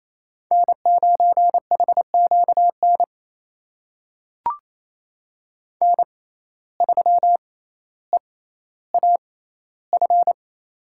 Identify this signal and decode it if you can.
Morse code, heard as a keyed tone, about 28 wpm, spaced as for 14 wpm, 700 Hz. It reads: N9HQN N3EAF
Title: USA Call Signs - 6x Character Spacing - 28wpm